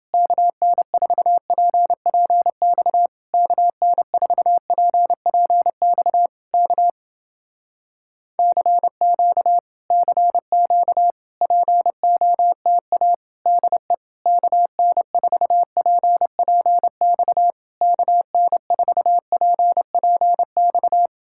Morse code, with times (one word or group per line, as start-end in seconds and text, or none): 0.1-3.1 KN4PPX
3.3-6.3 KN4PPX
6.5-6.9 K
8.4-9.6 CQ
9.9-11.1 CQ
11.4-13.1 POTA
13.5-13.9 DE
14.3-17.5 KN4PPX
17.8-21.1 KN4PPX